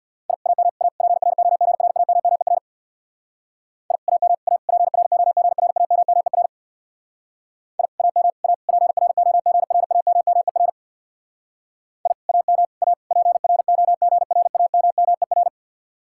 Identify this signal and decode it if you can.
Morse code, heard as a keyed tone, about 61 wpm, 700 Hz. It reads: I AM A PROGRAMMER I AM A PROGRAMMER I AM A PROGRAMMER I AM A PROGRAMMER
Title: Speed-Racing Common QSO Phrases Encore - 50wpm